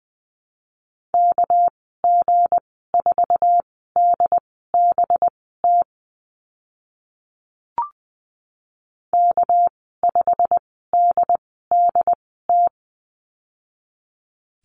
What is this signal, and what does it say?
Morse code, heard as a keyed tone, about 20 wpm, spaced as for 14 wpm, 700 Hz. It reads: KG4DBT K5DDT
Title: 207 Lesson 33g - Letter j - Call Signs 20wpm